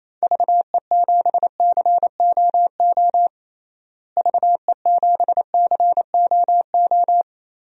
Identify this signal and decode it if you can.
Morse code, heard as a keyed tone, about 28 wpm, 700 Hz. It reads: VE7COO VE7COO
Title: DXCC-100 Call Signs - 6 Characters - Repeated 3x - 28wpm